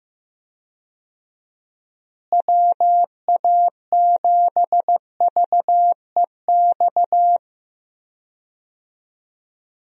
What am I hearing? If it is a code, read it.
Morse code, heard as a keyed tone, about 15 wpm, 700 Hz. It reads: WA7VEX
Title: USA Call Signs - 6 Characters - Repeated 2x - 15wpm